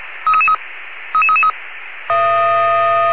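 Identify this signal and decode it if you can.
Five-tone selective call (ZVEI), heard as quick runs of five tones, tones of about 70 ms. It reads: 33333
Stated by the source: ZVEI five-tone selective call received on SDR.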